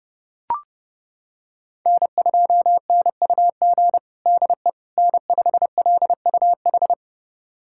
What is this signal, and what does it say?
Morse code, heard as a keyed tone, about 30 wpm, 700 Hz. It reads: N2NUG DE N5LUH
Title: USA Callsign DE Callsign - 5 Characters - 30wpm